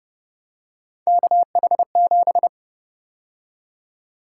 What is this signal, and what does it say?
Morse code, heard as a keyed tone, about 31 wpm, 700 Hz. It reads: KH7
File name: Speed-Racing - Three Letter-Number Combinations - 25wpm